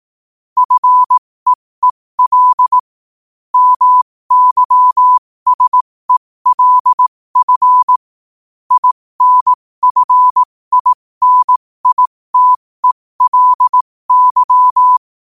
Morse code, heard as a keyed tone, about 18 wpm, spaced as for 15 wpm, 1000 Hz. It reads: FEEL MYSELF INFINITELY